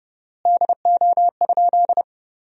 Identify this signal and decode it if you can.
Morse code, heard as a keyed tone, about 32 wpm, 700 Hz. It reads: DO?
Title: 4x Word-Spacing - Sentences from Top 100 Words - 30wpm